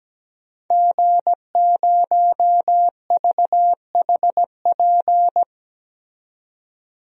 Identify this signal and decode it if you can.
Morse code, heard as a keyed tone, about 17 wpm, 700 Hz. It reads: G0VHP
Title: DXCC-100 Call Signs - 5 Characters - 17wpm